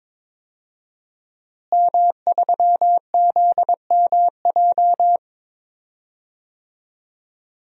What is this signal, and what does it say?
Morse code, heard as a keyed tone, about 21 wpm, 700 Hz. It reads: M3ZMJ
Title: DXCC-100 Call Signs - 5 Characters - Repeated 3x - 22wpm